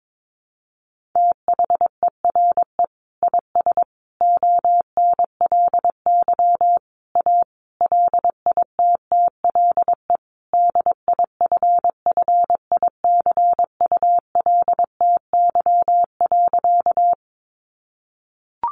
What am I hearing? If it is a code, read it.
Morse code, heard as a keyed tone, about 22 wpm, 700 Hz. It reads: THERE IS ONLY A LITTLE DIFFICULTY.
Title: Sentences from Top 1500 Words 22wpm